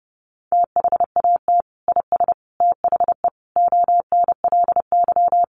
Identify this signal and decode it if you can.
Morse code, heard as a keyed tone, about 30 wpm, 700 Hz. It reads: THAT IS THE ONLY
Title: Sentences from Top 800 Words 30wpm